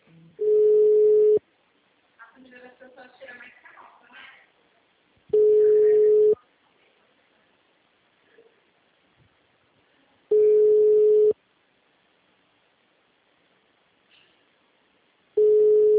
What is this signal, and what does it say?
Ringback tone (Europe, 425 Hz), with speech.
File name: out- -2006- - - .42361